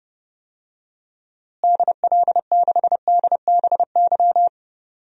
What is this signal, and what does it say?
Morse code, heard as a keyed tone, about 30 wpm, 700 Hz. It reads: DL6DBY
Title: DXCC-100 Call Signs - 6 Characters - Repeated 3x - 30wpm